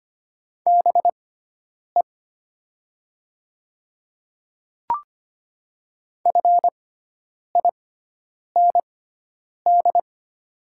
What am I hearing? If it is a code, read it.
Morse code, heard as a keyed tone, about 25 wpm, spaced as for 9 wpm, 700 Hz. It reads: BE FIND
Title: 6x Character Spacing - Top 200 Words - 25wpm